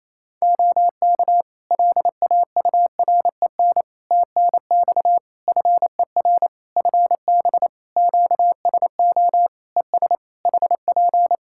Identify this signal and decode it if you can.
Morse code, heard as a keyed tone, about 27 wpm, 700 Hz. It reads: OK LAUREN TNX FER FB QSO ES HP
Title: Practice QSO - Exchange 4 - Ending - 28wpm